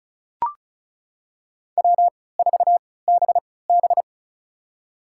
Morse code, heard as a keyed tone, about 34 wpm, spaced as for 20 wpm, 700 Hz. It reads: W4BB
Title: USA Call Signs - 4 Characters - 3x Character Spacing - 35wpm